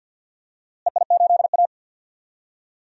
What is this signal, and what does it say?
Morse code, heard as a keyed tone, about 48 wpm, 700 Hz. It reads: EI8A
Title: DXCC-100 Call Signs - 4 Characters - Repeated 2x - 50wpm